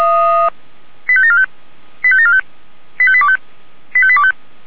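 Five-tone selective call (ZVEI), heard as quick runs of five tones, tones of about 70 ms. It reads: 85745 85745 85825 85825
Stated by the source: ZVEI five-tone selective call received on SDR, 464 channel path.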